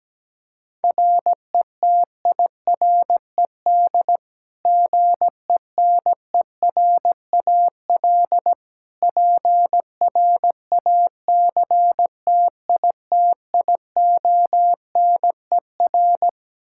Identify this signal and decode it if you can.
Morse code, heard as a keyed tone, about 17 wpm, 700 Hz. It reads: RETIRED GENERAL PRACTITIONER
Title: Practice QSO - Exchange 3 - More Info - 17wpm